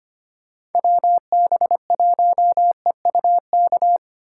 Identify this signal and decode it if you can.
Morse code, heard as a keyed tone, about 31 wpm, 700 Hz. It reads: WB1EUK
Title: Speed Racing - USA 2x3 Call Signs 25wpm